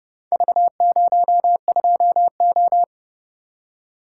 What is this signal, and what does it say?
Morse code, heard as a keyed tone, about 29 wpm, 700 Hz. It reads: V02O